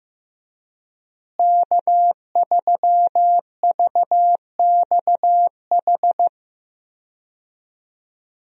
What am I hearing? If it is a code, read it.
Morse code, heard as a keyed tone, about 15 wpm, 700 Hz. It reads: K3VXH
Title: USA Call Signs - 5 Characters - Repeated 2x - 15wpm